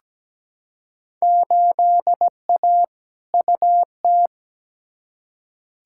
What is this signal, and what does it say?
Morse code, heard as a keyed tone, about 17 wpm, 700 Hz. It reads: 8A UT
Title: Field Day Exchanges Encore - 17wpm